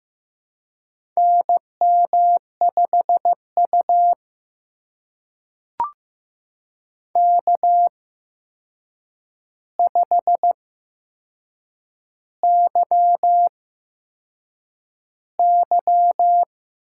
Morse code, heard as a keyed tone, about 15 wpm, spaced as for 6 wpm, 700 Hz. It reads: NM5U K5YY
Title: USA Call Signs - 4 Characters - 8x Character Spacing - 15wpm